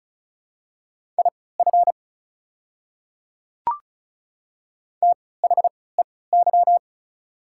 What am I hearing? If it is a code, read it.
Morse code, heard as a keyed tone, about 36 wpm, spaced as for 20 wpm, 690 Hz. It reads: IF THEY
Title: 3x Character Spacing - Top 100 Words - 35wpm